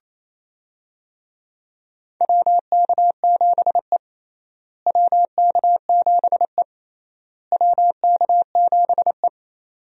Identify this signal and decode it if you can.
Morse code, heard as a keyed tone, about 28 wpm, 700 Hz. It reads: WK7E WK7E WK7E
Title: USA Call Signs - 4 Characters - Repeated 3x - 28wpm